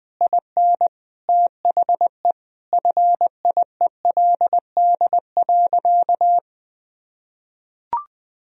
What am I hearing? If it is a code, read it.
Morse code, heard as a keyed tone, about 20 wpm, 700 Hz. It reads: IN THE FIELD.
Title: Sentences from Top 1000 Words - Review Part 4 - 20wpm